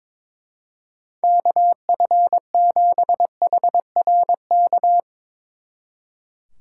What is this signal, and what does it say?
Morse code, heard as a keyed tone, about 22 wpm, 700 Hz. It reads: KF7HRK
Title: Speed Racing - USA 2x3 Call Signs 22wpm